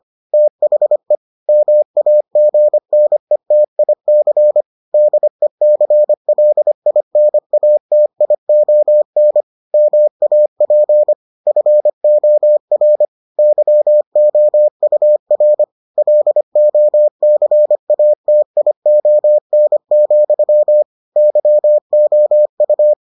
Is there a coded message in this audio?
Yes: Morse code from 0.3 to 22.9 s, heard as a keyed tone, about 25 wpm, 600 Hz. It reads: THE MAGNETIC DECLINATION MAP FOR YOUR LOCATION, YOU